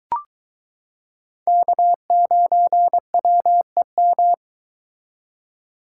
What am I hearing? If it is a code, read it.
Morse code, heard as a keyed tone, about 23 wpm, 700 Hz. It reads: K9WEM